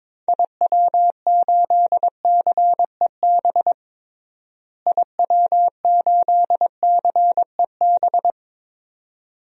Morse code, heard as a keyed tone, about 22 wpm, 700 Hz. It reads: IW8CEB IW8CEB
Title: DXCC-100 Call Signs - 6 Characters - Repeated 2x - 22wpm